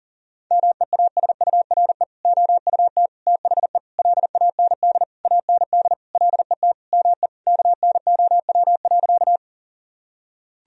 Morse code, heard as a keyed tone, about 40 wpm, 700 Hz. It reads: MEASURE OUT THE LAND AND LET ME KNOW.
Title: Sentences from Top 1000 Words - Review Part 2 - 40wpm